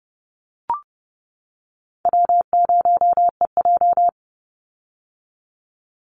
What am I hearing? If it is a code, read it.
Morse code, heard as a keyed tone, about 30 wpm, 700 Hz. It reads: W0EJ
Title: Easy Call Signs - 30wpm